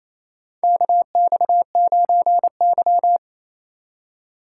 Morse code, heard as a keyed tone, about 28 wpm, 700 Hz. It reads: KX9Y
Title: USA Call Signs - 4 Characters - 2x Character Spacing - 28wpm